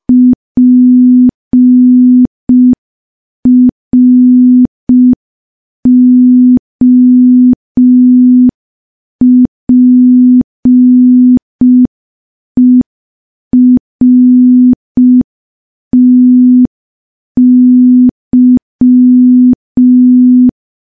Morse code, heard as a keyed tone, about 5 wpm, 250 Hz. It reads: PROPERTY
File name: morse / morse-training-05WPM